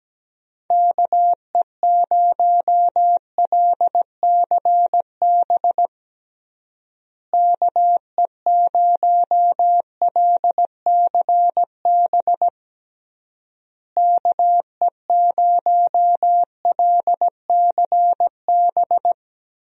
Morse code, heard as a keyed tone, about 17 wpm, 700 Hz. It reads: KE0LCB KE0LCB KE0LCB
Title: USA Call Signs - 6 Characters - Repeated 3x - 17wpm